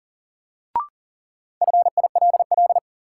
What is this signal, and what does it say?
Morse code, heard as a keyed tone, about 40 wpm, 700 Hz. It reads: FILL
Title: Sentences from Top 1000 Words - Review Part 3 - New Word Drill - 40wpm